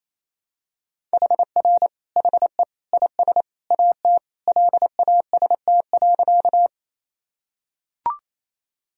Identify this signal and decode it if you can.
Morse code, heard as a keyed tone, about 28 wpm, 700 Hz. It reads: HR HE IS AT LAST.
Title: Sentences from Top 300 Words and Abbreviations 28wpm